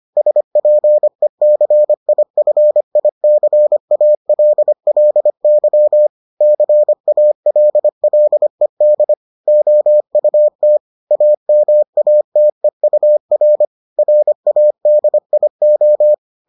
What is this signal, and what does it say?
Morse code, heard as a keyed tone, about 25 wpm, 600 Hz. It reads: SPECIFICALLY CALLED OUT AMATEUR RADIO